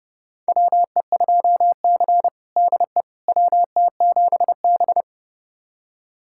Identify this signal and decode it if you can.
Morse code, heard as a keyed tone, about 30 wpm, 700 Hz. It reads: WE2C DE WT7B
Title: USA Callsign DE Callsign - 4 Characters - 30wpm